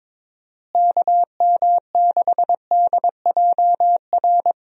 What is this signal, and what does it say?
Morse code, heard as a keyed tone, about 22 wpm, 700 Hz. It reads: KM6DJR
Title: USA Call Signs - 6 Characters - Repeated 3x - 22wpm